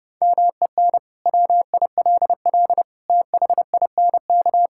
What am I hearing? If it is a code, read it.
Morse code, heard as a keyed tone, about 30 wpm, 700 Hz. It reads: MEN WILL THINK
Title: Sentences from Top 1000 Words - Review Part 5 - 30wpm